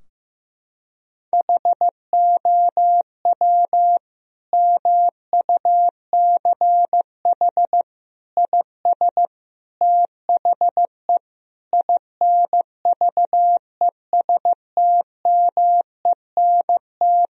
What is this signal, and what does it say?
Morse code, heard as a keyed tone, about 16 wpm, 700 Hz. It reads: HOW MUCH IS THE INVESTMENT
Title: Sentences from Top 1000 Words - Review Part 4 - 15wpm